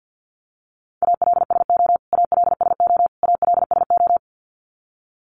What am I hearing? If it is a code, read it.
Morse code, heard as a keyed tone, about 50 wpm, 700 Hz. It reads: ALSO ALSO ALSO
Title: Top 500 Words Repeated 50wpm